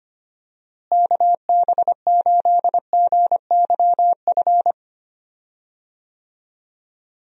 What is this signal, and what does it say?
Morse code, heard as a keyed tone, about 25 wpm, 700 Hz. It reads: KB8GYF
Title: USA Call Signs - 6 Characters - Repeated 3x - 25wpm